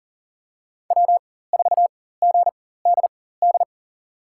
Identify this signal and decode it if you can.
Morse code, heard as a keyed tone, about 40 wpm, spaced as for 19 wpm, 700 Hz. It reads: W4GDD
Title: USA Call Signs - 4x Character Spacing - 40wpm